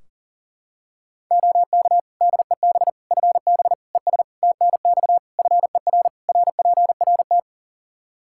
Morse code, heard as a keyed tone, about 40 wpm, 700 Hz. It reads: OK DEB FB ES TNX FER RPRT
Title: Practice QSO - Exchange 2 - Info - 40wpm